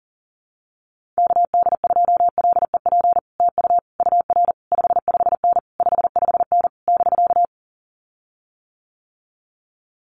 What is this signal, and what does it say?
Morse code, heard as a keyed tone, about 40 wpm, 700 Hz. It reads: KD2LEP TU UR 55N 55N <BK>
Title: Common QSO Phrases Encore 40wpm